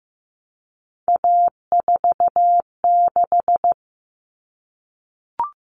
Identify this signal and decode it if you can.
Morse code, heard as a keyed tone, about 15 wpm, 700 Hz. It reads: A46
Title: Common Cut Numbers - 3-Digit 15wpm